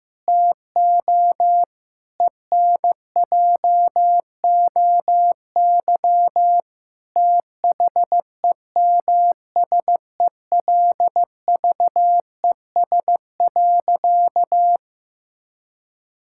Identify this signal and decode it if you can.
Morse code, heard as a keyed tone, about 15 wpm, 700 Hz. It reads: TO ENJOY THEMSELVES.
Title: Sentences from Top 1000 Words - Review Part 5 - 15wpm